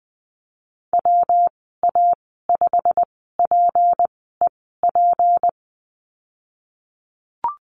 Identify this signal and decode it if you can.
Morse code, heard as a keyed tone, about 20 wpm, spaced as for 15 wpm, 700 Hz. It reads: WA5PEP
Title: 173 Lesson 28g - Number 9 - Call Signs 20wpm